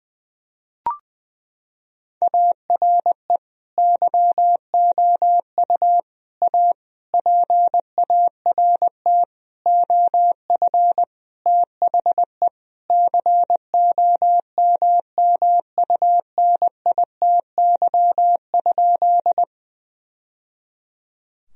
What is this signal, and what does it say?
Morse code, heard as a keyed tone, about 21 wpm, 700 Hz. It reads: ARE YOU A PART OF THE COMMUNITY?
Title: Sentences from Top 1000 Words - Review Part 4 - 20wpm